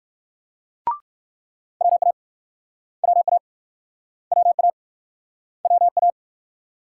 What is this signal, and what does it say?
Morse code, heard as a keyed tone, about 60 wpm, 700 Hz. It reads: WA WA WA WA